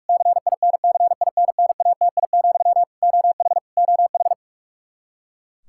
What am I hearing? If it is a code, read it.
Morse code, heard as a keyed tone, about 41 wpm, 700 Hz. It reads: CINCINNATI, OH OH
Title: Practice QSO - Exchange 1 - Beginning - 45wpm